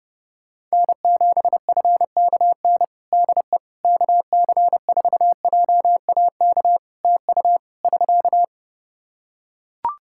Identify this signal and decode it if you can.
Morse code, heard as a keyed tone, about 32 wpm, 700 Hz. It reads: N7FKN DE KC4JAK TU <SK>
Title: Practice QSO - Exchange 4 - Ending - 30wpm